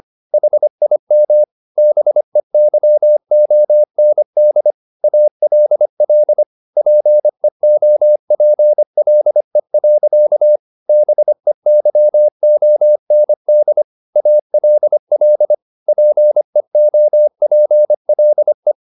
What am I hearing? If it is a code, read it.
Morse code, heard as a keyed tone, about 25 wpm, 600 Hz. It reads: HIM BEYOND ALL PEOPLE. BEYOND ALL PEOPLE
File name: the old man and the sea part 1 25wpm